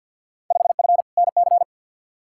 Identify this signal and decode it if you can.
Morse code, heard as a keyed tone, about 52 wpm, 700 Hz. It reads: 5F NC
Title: Field Day Exchanges Encore - 50wpm